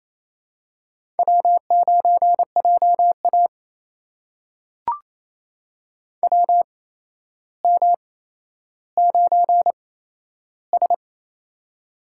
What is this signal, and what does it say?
Morse code, heard as a keyed tone, about 28 wpm, spaced as for 12 wpm, 700 Hz. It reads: W9JA WM9S
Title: USA Call Signs - 4 Characters - 8x Character Spacing - 28wpm